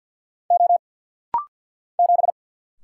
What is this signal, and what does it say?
Morse code, heard as a keyed tone, about 50 wpm, 700 Hz. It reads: O 7
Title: Single Letter-Number Alt - Rapid-Fire 50wpm